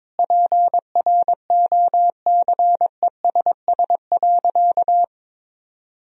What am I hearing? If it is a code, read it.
Morse code, heard as a keyed tone, about 22 wpm, 700 Hz. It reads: PROCESS.
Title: Sentences from Top 1000 Words - Review Part 4 - 22wpm